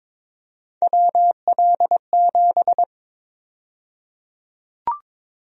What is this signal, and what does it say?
Morse code, heard as a keyed tone, about 22 wpm, 700 Hz. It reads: WL7
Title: USA Call Sign Prefixes - 22wpm